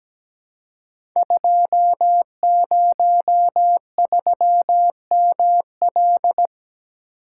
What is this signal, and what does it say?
Morse code, heard as a keyed tone, about 17 wpm, 700 Hz. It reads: 203ML
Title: Code Groups - 5 Letter:Numbers - 17wpm